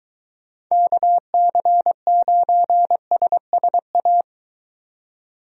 Morse code, heard as a keyed tone, about 20 wpm, 700 Hz. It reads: KC9SSA